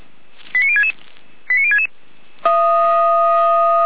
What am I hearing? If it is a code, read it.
Five-tone selective call (ZVEI), heard as quick runs of five tones, tones of about 70 ms. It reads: 89066 89066